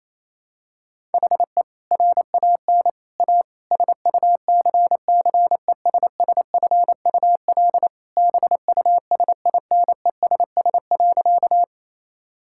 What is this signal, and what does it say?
Morse code, heard as a keyed tone, about 27 wpm, 700 Hz. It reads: HE RAN A SUCCESSFUL BUSINESS.
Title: Sentences from Top 1000 Words - Review Part 6 - 28wpm